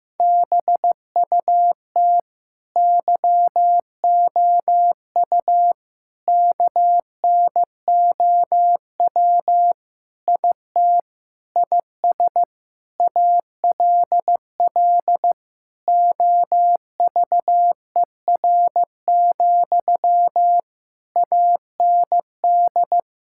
Morse code, heard as a keyed tone, about 15 wpm, 700 Hz. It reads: BUT YOU KNOW IT IS ALL OVER, AND
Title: Sentences from Top 1600 Words 15wpm